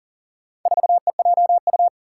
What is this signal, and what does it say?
Morse code, heard as a keyed tone, about 41 wpm, 700 Hz. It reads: 4EJU
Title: Four Challenging Letter-Number Combinations 40wpm